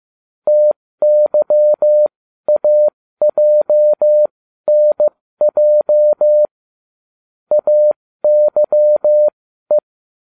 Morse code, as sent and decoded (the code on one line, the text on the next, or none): - -.-- .- .--- -. .--- / .- -.-- .
TYAJNJ AYE